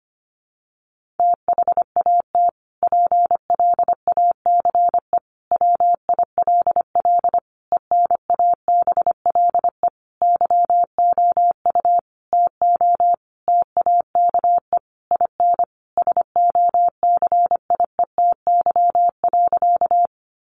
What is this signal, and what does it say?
Morse code, heard as a keyed tone, about 25 wpm, 700 Hz. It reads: THAT PLACE WILL ENABLE YOU TO TAKE IN SOCIETY.
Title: Sentences from Top 1500 Words 25wpm